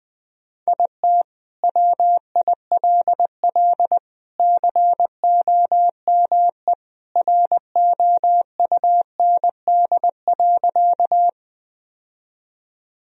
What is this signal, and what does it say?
Morse code, heard as a keyed tone, about 21 wpm, 700 Hz. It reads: IT WILL COME ROUND.
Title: Sentences from Top 1000 Words - Review Part 1 - 20wpm